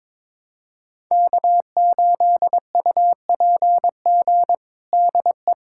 Morse code, heard as a keyed tone, about 22 wpm, 700 Hz. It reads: K8UPG DE
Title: USA Callsign DE Callsign 22wpm